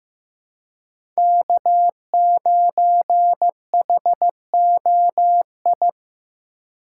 Morse code, heard as a keyed tone, about 15 wpm, 700 Hz. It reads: K9HOI
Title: USA Call Signs - 5 Characters - 3x Character Spacing - 15wpm